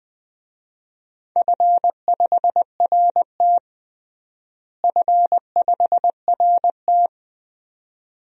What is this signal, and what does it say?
Morse code, heard as a keyed tone, about 21 wpm, 700 Hz. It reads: F5RT F5RT
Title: DXCC-100 Call Signs - 4 Characters - Repeated 2x - 20wpm